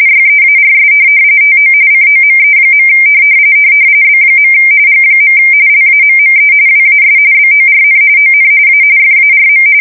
RTTY, about 45.45 baud. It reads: YEARS TO COME. B. G. LAMME WAS BORN ON A FARM